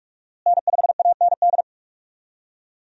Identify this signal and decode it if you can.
Morse code, heard as a keyed tone, about 48 wpm, 700 Hz. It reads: N5AND
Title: USA Call Signs - 5 Characters - Repeated 2x - 45wpm